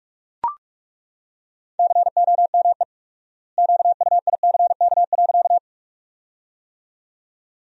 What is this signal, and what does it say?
Morse code, heard as a keyed tone, about 41 wpm, 700 Hz. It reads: COME QUICK.
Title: 4x Word-Spacing - Sentences from Top 500 Words - 45wpm